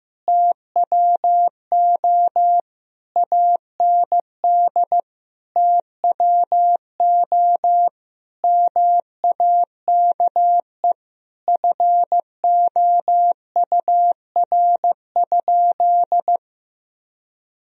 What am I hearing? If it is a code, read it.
Morse code, heard as a keyed tone, about 15 wpm, 700 Hz. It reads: TWO AND TWO MAKE FOUR?
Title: Sentences from Top 1000 Words - Review Part 1 - 15wpm